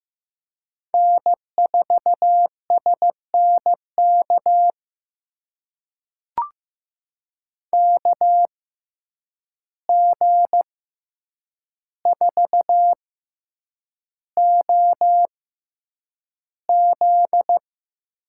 Morse code, heard as a keyed tone, about 15 wpm, spaced as for 8 wpm, 700 Hz. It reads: N4SNK KG4OZ
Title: USA Call Signs - 6x Character Spacing - 15wpm